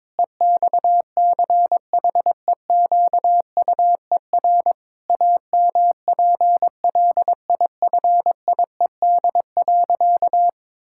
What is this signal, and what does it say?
Morse code, heard as a keyed tone, about 22 wpm, 700 Hz. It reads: EXCHEQUER AMPLIFIED.
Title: Pangram Challenge v2 - 22wpm